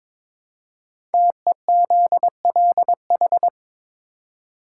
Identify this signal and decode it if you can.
Morse code, heard as a keyed tone, about 22 wpm, 700 Hz. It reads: TEZLH
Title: Code Groups - 5 Letters - 22wpm